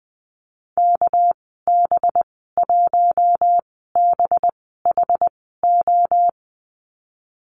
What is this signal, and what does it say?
Morse code, heard as a keyed tone, about 19 wpm, spaced as for 14 wpm, 700 Hz. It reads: KB1BHO
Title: 245 Lesson 39g - Letter z - Call Signs 20wpm